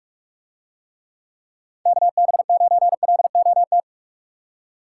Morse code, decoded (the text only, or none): KB9LOT